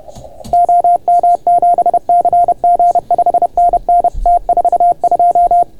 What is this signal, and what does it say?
Morse code, heard as a keyed tone, about 31 wpm, 690 Hz. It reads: OM7CG 5NN T42